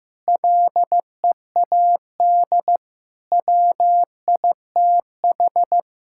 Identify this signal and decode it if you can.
Morse code, heard as a keyed tone, about 15 wpm, 700 Hz. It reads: LEAD WITH
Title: Sentences from Top 1000 Words - Review Part 5 - 15wpm